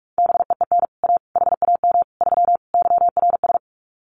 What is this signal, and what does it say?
Morse code, heard as a keyed tone, about 44 wpm, 700 Hz. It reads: BEEN A HAM 3 YRS
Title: Common QSO Phrases Encore 45wpm